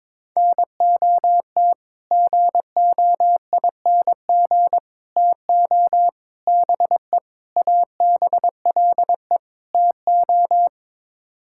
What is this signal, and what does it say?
Morse code, heard as a keyed tone, about 22 wpm, 700 Hz. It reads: NOT GOING TO BE ABLE TO
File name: practice / 22wpm / Sets of 6 Words - 22wpm